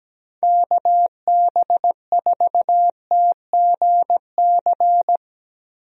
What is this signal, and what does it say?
Morse code, heard as a keyed tone, about 17 wpm, 700 Hz. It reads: KB4TGC
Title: USA Call Signs - 6 Characters - Repeated 2x - 17wpm